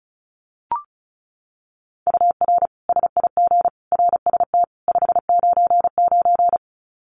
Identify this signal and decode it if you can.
Morse code, heard as a keyed tone, about 34 wpm, 700 Hz. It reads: UR SIG RST 599
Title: Common QSO Phrases 35wpm